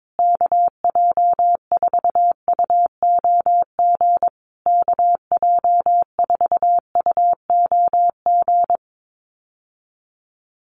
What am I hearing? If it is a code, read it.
Morse code, heard as a keyed tone, about 22 wpm, 700 Hz. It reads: KJ4UOG KJ4UOG
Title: USA Call Signs Repeated 3x 22wpm